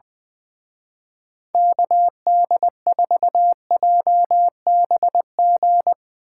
Morse code, heard as a keyed tone, about 21 wpm, 700 Hz. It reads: KD4JBG
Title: Speed Racing - USA 2x3 Call Signs 20wpm